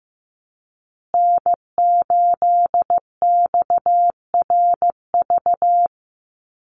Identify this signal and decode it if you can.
Morse code, heard as a keyed tone, about 15 wpm, 700 Hz. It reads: N8XRV
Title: USA Call Signs Repeated 3x 15wpm